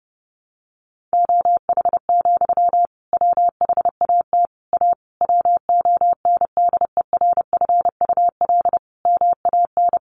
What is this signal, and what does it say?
Morse code, heard as a keyed tone, about 30 wpm, 700 Hz. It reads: OH, WHAT A WONDERFUL MAN
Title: Sentences from Top 1100 Words 30wpm